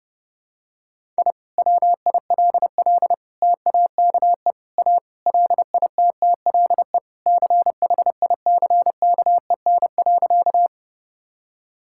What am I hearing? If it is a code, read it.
Morse code, heard as a keyed tone, about 32 wpm, 700 Hz. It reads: I WILL TAKE A LITTLE CHICKEN.
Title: Sentences from Top 1000 Words - Review Part 5 - 30wpm